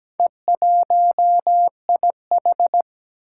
Morse code, heard as a keyed tone, about 17 wpm, 700 Hz. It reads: E1IH
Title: Four Challenging Letter-Number Combinations 17wpm